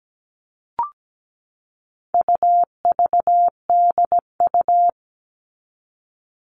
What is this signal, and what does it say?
Morse code, heard as a keyed tone, about 17 wpm, 700 Hz. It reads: UVDU
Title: All Cut Numbers - 4-Digit 17wpm